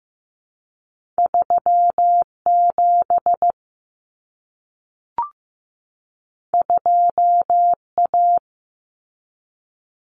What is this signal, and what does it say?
Morse code, heard as a keyed tone, about 15 wpm, 700 Hz. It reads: 37 2A
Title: Common Cut Numbers - 2-Digit 15wpm